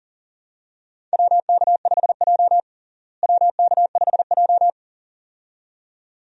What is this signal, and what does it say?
Morse code, heard as a keyed tone, about 40 wpm, 700 Hz. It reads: WK5J WK5J
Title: USA Call Signs - 4 Characters - Repeated 2x - 40wpm